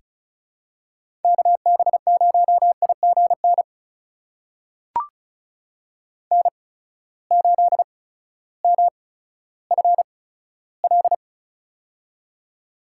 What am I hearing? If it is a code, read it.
Morse code, heard as a keyed tone, about 34 wpm, spaced as for 16 wpm, 700 Hz. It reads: KB0IGN N8MFL
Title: USA Call Signs - 8x Character Spacing - 35wpm